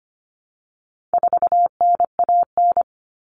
Morse code, heard as a keyed tone, about 25 wpm, 700 Hz. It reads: 4NAN
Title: Common Cut Numbers - 4-Digit 25wpm